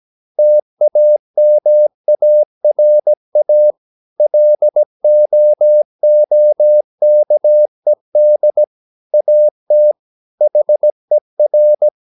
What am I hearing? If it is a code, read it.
Morse code, heard as a keyed tone, about 17 wpm, 600 Hz. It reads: TAMARA LOOKED AT HER
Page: Romanen Never af Ken Follet på engelsk.